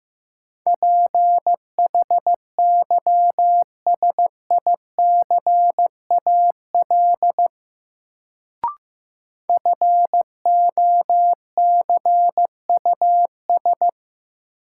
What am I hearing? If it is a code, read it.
Morse code, heard as a keyed tone, about 16 wpm, 700 Hz. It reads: PHYSICAL FOCUS
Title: Sentences from Top 1000 Words - Review Part 4 - New Word Drill - 15wpm